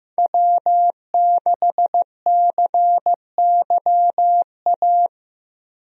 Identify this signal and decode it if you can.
Morse code, heard as a keyed tone, about 15 wpm, 700 Hz. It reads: W6CYA
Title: USA Call Signs - 5 Characters - Repeated 2x - 15wpm